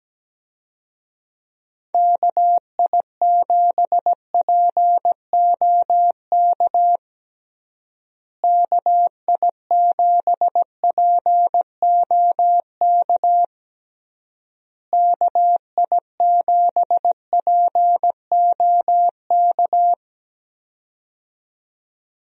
Morse code, heard as a keyed tone, about 17 wpm, 700 Hz. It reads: KI7POK KI7POK KI7POK
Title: USA Call Signs - 6 Characters - Repeated 3x - 17wpm